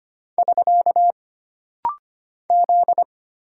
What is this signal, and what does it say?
Morse code, heard as a keyed tone, about 25 wpm, 700 Hz. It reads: <SK> Z
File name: Single Character (Letter:Number:Punctuation:Prosign) - Rapid-Fire - 25wpm